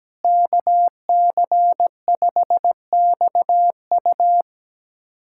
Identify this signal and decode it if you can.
Morse code, heard as a keyed tone, about 17 wpm, 700 Hz. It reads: KC5XU
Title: USA Callsign DE Callsign - 5 Characters - 17wpm